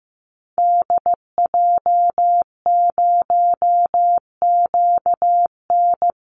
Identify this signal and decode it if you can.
Morse code, heard as a keyed tone, about 15 wpm, 700 Hz. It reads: DJ0QN
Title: Call Signs Repeated 2x - 15wpm